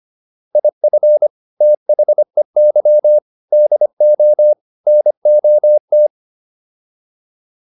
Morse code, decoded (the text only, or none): IF THEY DO NOT